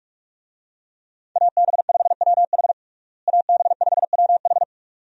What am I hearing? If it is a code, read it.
Morse code, heard as a keyed tone, about 46 wpm, 700 Hz. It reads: AB5WH AB5WH
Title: USA Call Signs - 5 Characters - Repeated 2x - 45wpm